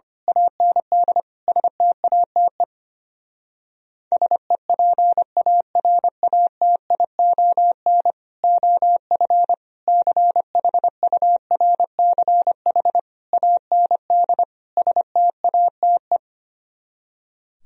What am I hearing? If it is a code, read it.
Morse code, heard as a keyed tone, about 24 wpm, 700 Hz. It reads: AND STATE SEPARATION OF CHURCH AND STATE
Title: Speed-Racing - Sets of 5 Words 25wpm